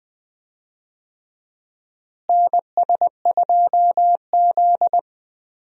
Morse code, heard as a keyed tone, about 20 wpm, 700 Hz. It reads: NS2Z
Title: USA Call Signs - 4 Characters - Repeated 2x - 20wpm